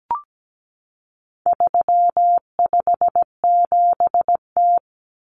Morse code, heard as a keyed tone, about 17 wpm, 700 Hz. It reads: 357T
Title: Common Cut Numbers - 4-Digit 17wpm